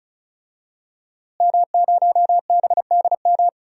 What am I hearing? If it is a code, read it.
Morse code, heard as a keyed tone, about 36 wpm, 700 Hz. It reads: M0BDM
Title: DXCC-100 Call Signs - 5 Characters - Repeated 3x - 35wpm